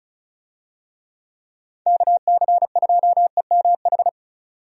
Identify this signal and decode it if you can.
Morse code, heard as a keyed tone, about 34 wpm, 700 Hz. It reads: KC2EMH
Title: USA Call Signs - 6 Characters - 35wpm